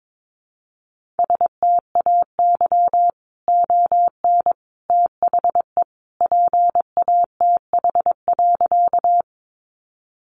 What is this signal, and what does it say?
Morse code, heard as a keyed tone, about 22 wpm, 700 Hz. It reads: STAY ON THE PATH.
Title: A Sound Of Thunder 22wpm